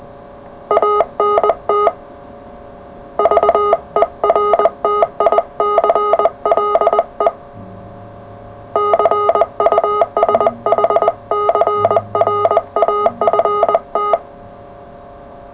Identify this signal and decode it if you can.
Morse code, heard as a keyed tone, about 20 wpm, 420 Hz, with noise. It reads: ANT VERTICLE CUSHCRAFT
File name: full / cw qso